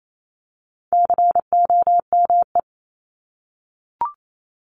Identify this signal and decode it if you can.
Morse code, heard as a keyed tone, about 28 wpm, 700 Hz. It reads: COME
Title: Top 300 Words 28wpm